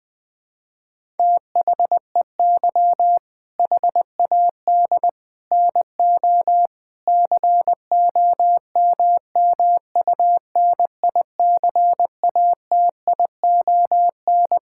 Morse code, heard as a keyed tone, about 20 wpm, 700 Hz. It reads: THEY HAD NO COMMUNICATION
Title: Sentences from Top 1000 Words - Review Part 6 - 20wpm